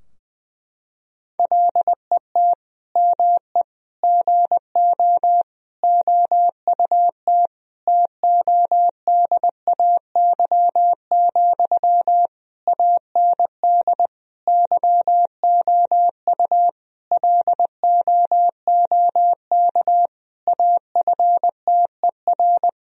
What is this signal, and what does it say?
Morse code, heard as a keyed tone, about 20 wpm, 700 Hz. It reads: LET ME GO OUT TODAY, AND YOU LOOK AFTER
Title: Sentences from Top 1000 Words - Review Part 4 - 20wpm